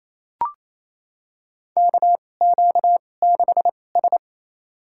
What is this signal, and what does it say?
Morse code, heard as a keyed tone, about 28 wpm, spaced as for 20 wpm, 700 Hz. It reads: KQ6S
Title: USA Call Signs - 4 Characters - 2x Character Spacing - 28wpm